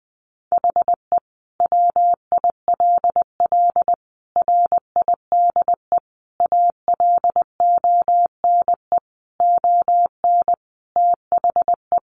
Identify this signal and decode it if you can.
Morse code, heard as a keyed tone, about 20 wpm, 700 Hz. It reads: HE WILL RIDE ALONE ON THE
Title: Sentences from Top 1400 Words 20wpm